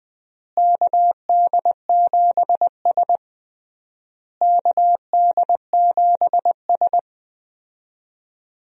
Morse code, heard as a keyed tone, about 20 wpm, 700 Hz. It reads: KD7S KD7S
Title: USA Call Signs - 4 Characters - Repeated 3x - 20wpm